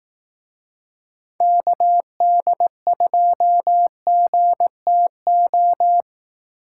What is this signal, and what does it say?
Morse code, heard as a keyed tone, about 17 wpm, 700 Hz. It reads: KD2GTO
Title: Speed Racing - USA 2x3 Call Signs 15wpm